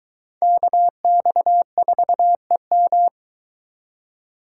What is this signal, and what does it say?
Morse code, heard as a keyed tone, about 24 wpm, 700 Hz. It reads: KX4EM